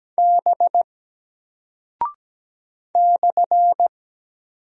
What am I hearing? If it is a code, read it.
Morse code, heard as a keyed tone, about 17 wpm, 700 Hz. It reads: B /
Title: Single Character (Letter:Number:Punctuation:Prosign) Alt 17wpm